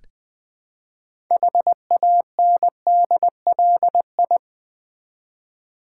Morse code, heard as a keyed tone, about 20 wpm, 700 Hz. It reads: HANDLI
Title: Build A Word - 3 Letters At A Time - 8-Letter Words - 20wpm